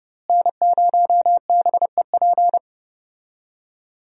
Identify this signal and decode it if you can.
Morse code, heard as a keyed tone, about 30 wpm, 700 Hz. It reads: N0BEP
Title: USA Call Signs - 5 Characters - Repeated 3x - 30wpm